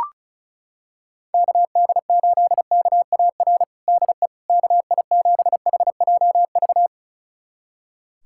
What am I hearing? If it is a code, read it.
Morse code, heard as a keyed tone, about 34 wpm, 700 Hz. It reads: KD8KAR DE KI7HJV
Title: USA Callsign DE Callsign - 6 Characters - 35wpm